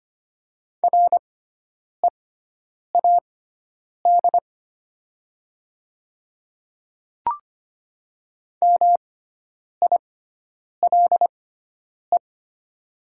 Morse code, heard as a keyed tone, about 25 wpm, spaced as for 9 wpm, 700 Hz. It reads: READ MILE
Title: 6x Character Spacing - Top 300 Words - 25wpm